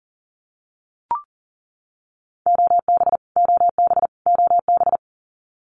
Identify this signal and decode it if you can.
Morse code, heard as a keyed tone, about 40 wpm, 700 Hz. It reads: OB OB OB
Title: Abbreviations Repeated 3x - 40wpm